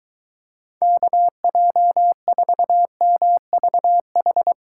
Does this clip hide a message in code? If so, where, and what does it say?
Yes: Morse code from 0.8 to 4.5 s, heard as a keyed tone, about 22 wpm, 700 Hz. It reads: KJ4MVH